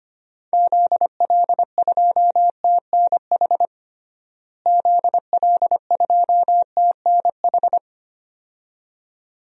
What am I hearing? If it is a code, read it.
Morse code, heard as a keyed tone, about 25 wpm, 700 Hz. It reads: ZL2TNH ZL2TNH
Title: DXCC-100 Call Signs - 6 Characters - Repeated 2x - 25wpm